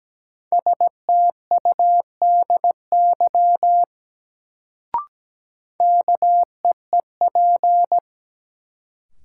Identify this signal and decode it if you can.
Morse code, heard as a keyed tone, about 17 wpm, 700 Hz. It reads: STUDY KEEP
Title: Sentences from Top 300 Words - New Word Drill - 17wpm